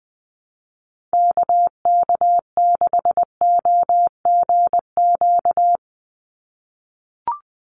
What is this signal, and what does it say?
Morse code, heard as a keyed tone, about 20 wpm, 700 Hz. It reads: KK6OGQ
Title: USA Call Signs 20wpm